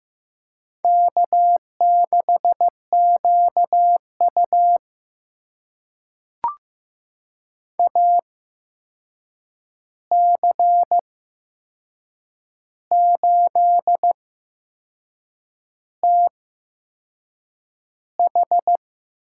Morse code, heard as a keyed tone, about 15 wpm, spaced as for 6 wpm, 700 Hz. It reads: K6QU AC8TH
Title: USA Call Signs - 8x Character Spacing - 15wpm